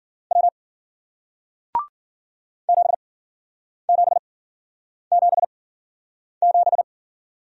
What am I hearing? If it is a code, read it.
Morse code, heard as a keyed tone, about 53 wpm, spaced as for 9 wpm, 700 Hz. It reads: W 7777